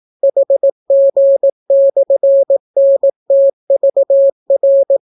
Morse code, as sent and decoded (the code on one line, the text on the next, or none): .... --. -..-. -. - ...- .-.
HG/NTVR